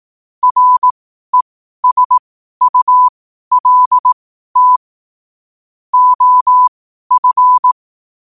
Morse code, heard as a keyed tone, about 18 wpm, spaced as for 12 wpm, 1000 Hz. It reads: RESULT OF
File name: qotdfiles / 12wpm